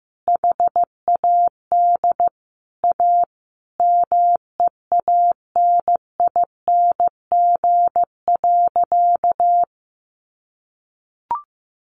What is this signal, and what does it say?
Morse code, heard as a keyed tone, about 15 wpm, 700 Hz. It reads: HAD A MEANING.
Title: Sentences from Top 1400 Words 15wpm